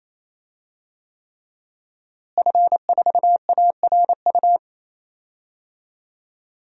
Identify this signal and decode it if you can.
Morse code, heard as a keyed tone, about 27 wpm, 700 Hz. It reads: F4ARU
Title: DXCC-100 Call Signs - 5 Characters - Repeated 3x - 28wpm